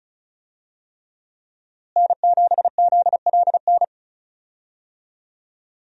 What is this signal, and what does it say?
Morse code, heard as a keyed tone, about 36 wpm, 700 Hz. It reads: N7ZLN
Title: USA Call Signs - 5 Characters - Repeated 3x - 35wpm